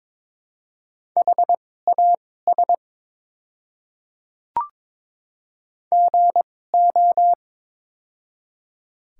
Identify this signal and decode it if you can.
Morse code, heard as a keyed tone, about 22 wpm, spaced as for 16 wpm, 700 Hz. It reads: HAS GO
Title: 2x Character Spacing - Top 300 Words - 22wpm